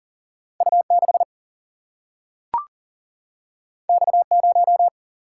Morse code, heard as a keyed tone, about 39 wpm, 700 Hz. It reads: U6 X0
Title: Single-Letter Single-Number 40wpm